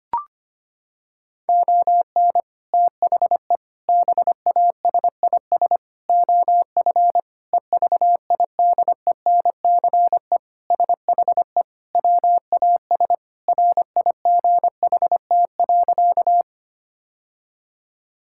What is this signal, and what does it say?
Morse code, heard as a keyed tone, about 25 wpm, 700 Hz. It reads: ON THE BASIS OF EVIDENCE SHE WAS RIGHT.
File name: practice / 25wpm / Sentences with 4-Word Phrases - 25wpm